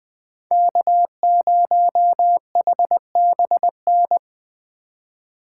Morse code, heard as a keyed tone, about 20 wpm, 700 Hz. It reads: K0HBN
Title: Speed Racing - USA 1x3 Call Signs 17wpm